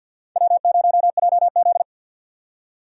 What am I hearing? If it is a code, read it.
Morse code, heard as a keyed tone, about 47 wpm, 700 Hz. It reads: W0JZ
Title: USA Call Signs - 4 Characters - 50wpm